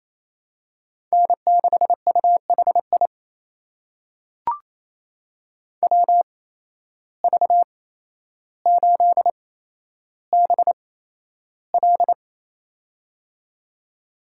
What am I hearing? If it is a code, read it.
Morse code, heard as a keyed tone, about 27 wpm, spaced as for 12 wpm, 700 Hz. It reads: N6UHI WV8BL
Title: USA Call Signs - 5 Characters - 8x Character Spacing - 28wpm